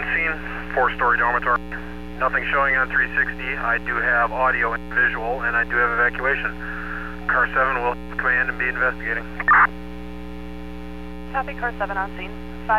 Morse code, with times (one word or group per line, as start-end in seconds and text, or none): none